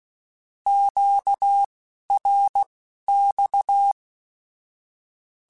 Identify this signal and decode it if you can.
Morse code, heard as a keyed tone, about 16 wpm, spaced as for 11 wpm, 800 Hz. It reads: QRX
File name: (spaced) / Q-Calls Spaced-mp3s / Q Calls sp Once @15